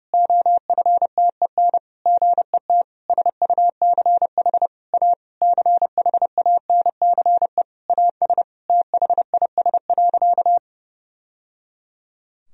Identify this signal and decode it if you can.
Morse code, heard as a keyed tone, about 30 wpm, 700 Hz. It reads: OFTEN GET SUCH A CHANCE AS THIS.
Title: Sentences from Top 1000 Words - Review Part 1 - 30wpm